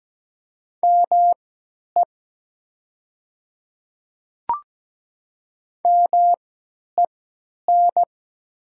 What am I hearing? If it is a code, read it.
Morse code, heard as a keyed tone, about 17 wpm, spaced as for 10 wpm, 700 Hz. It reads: ME MEN
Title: 3x Character Spacing - Top 200 Words - 17wpm